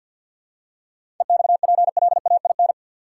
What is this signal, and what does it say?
Morse code, heard as a keyed tone, about 53 wpm, 700 Hz. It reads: EXPLAIN
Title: Seven Letter Words - 50wpm